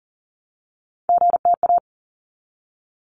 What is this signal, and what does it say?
Morse code, heard as a keyed tone, about 40 wpm, 700 Hz. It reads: GTA
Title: ARRL Sections 40wpm